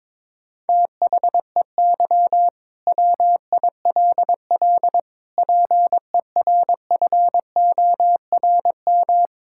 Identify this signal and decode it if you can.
Morse code, heard as a keyed tone, about 22 wpm, 700 Hz. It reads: THEY WILL PERFORM
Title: Sentences from Top 1000 Words - Review Part 5 - 22wpm